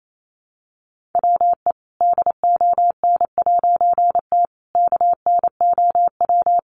Morse code, heard as a keyed tone, about 28 wpm, 700 Hz. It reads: WE DON'T KNOW
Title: A Sound Of Thunder 28wpm